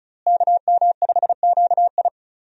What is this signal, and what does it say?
Morse code, heard as a keyed tone, about 36 wpm, 700 Hz. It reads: KM5QI
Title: USA Call Signs - 5 Characters - Repeated 3x - 35wpm